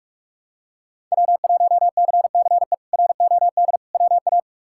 Morse code, heard as a keyed tone, about 48 wpm, 700 Hz. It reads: W1CCE ROD WA
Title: K1USN SST QSO Practice - 45wpm